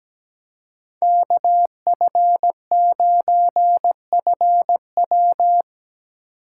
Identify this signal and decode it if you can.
Morse code, heard as a keyed tone, about 17 wpm, 700 Hz. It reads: KF9FW
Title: USA Call Signs - 5 Characters - 3x Character Spacing - 17wpm